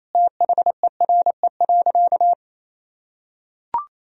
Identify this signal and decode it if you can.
Morse code, heard as a keyed tone, about 28 wpm, 700 Hz. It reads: THERE.
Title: Sentences from Top 1000 Words - Review Part 1 - 28wpm